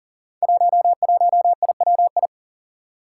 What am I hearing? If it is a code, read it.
Morse code, heard as a keyed tone, about 41 wpm, 700 Hz. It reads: 11IWI
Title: Five Challenging Letter-Number Combinations 40wpm